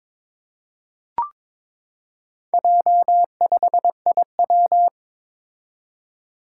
Morse code, heard as a keyed tone, about 22 wpm, 700 Hz. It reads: J5IW
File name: practice / 22wpm / Four Letter-Number Combinations - 22wpm